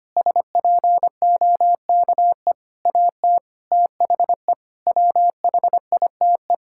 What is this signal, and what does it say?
Morse code, heard as a keyed tone, about 25 wpm, 700 Hz. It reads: SPOKE AT THE WHITE
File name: practice / 25wpm / Sentences with 4-Word Phrases - 25wpm